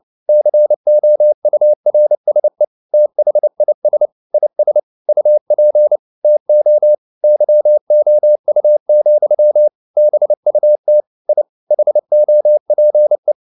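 Morse code, heard as a keyed tone, about 29 wpm, 600 Hz. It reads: COURSE THIS IS UP TO YOU, BUT I HOPE
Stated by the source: Morse code at 29 wpm, sending Ken Follett's novel Never in English, from CW med Gnister.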